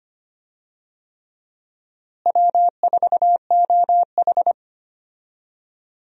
Morse code, heard as a keyed tone, about 25 wpm, 700 Hz. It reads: W4OH
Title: USA Call Signs - 4 Characters - Repeated 3x - 25wpm